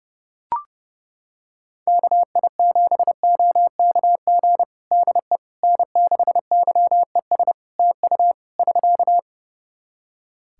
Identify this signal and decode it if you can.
Morse code, heard as a keyed tone, about 32 wpm, 700 Hz. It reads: KI7OKG DE N6YES TU <SK>
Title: Practice QSO - Exchange 4 - Ending - 30wpm